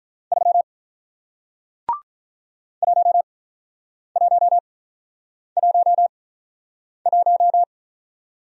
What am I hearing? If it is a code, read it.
Morse code, heard as a keyed tone, about 48 wpm, spaced as for 9 wpm, 700 Hz. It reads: 3 1111